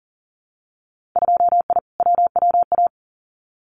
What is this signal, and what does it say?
Morse code, heard as a keyed tone, about 40 wpm, 700 Hz. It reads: 2I WWA
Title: Winter Field Day Exchanges Encore 40wpm